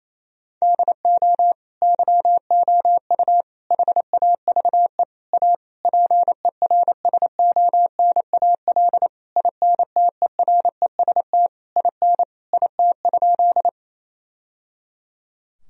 Morse code, heard as a keyed tone, about 28 wpm, 700 Hz. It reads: DO YOU HAVE A PERSONAL INTEREST IN IT?
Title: Sentences from Top 1000 Words - Review Part 4 - 28wpm